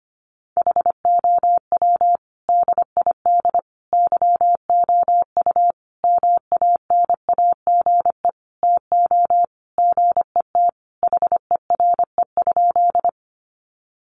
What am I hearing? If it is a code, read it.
Morse code, heard as a keyed tone, about 25 wpm, 700 Hz. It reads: HOW DID YOU MANAGE TO GET HERE?
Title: Sentences from Top 1400 Words 25wpm